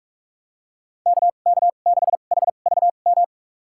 Morse code, heard as a keyed tone, about 41 wpm, spaced as for 31 wpm, 690 Hz. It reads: KK6HVM
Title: USA Call Signs - 6 Characters - 2x Character Spacing - 45wpm